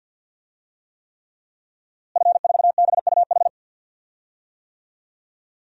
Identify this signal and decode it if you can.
Morse code, heard as a keyed tone, about 52 wpm, 700 Hz. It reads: F4BUH
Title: DXCC-100 Call Signs - 5 Characters - Repeated 3x - 50wpm